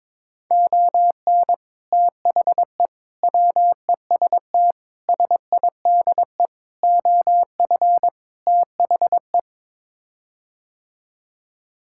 Morse code, heard as a keyed tone, about 21 wpm, 700 Hz. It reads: ON THE WEST SIDE OF THE
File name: practice / 22wpm / Sets of 6 Words - 22wpm